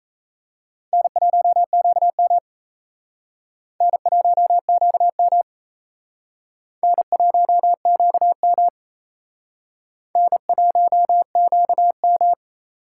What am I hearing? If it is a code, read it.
Morse code, heard as a keyed tone, about 34 wpm, 700 Hz. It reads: N1QM N1QM N1QM N1QM
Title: Speed Racing - USA 1x2 Call Signs 28wpm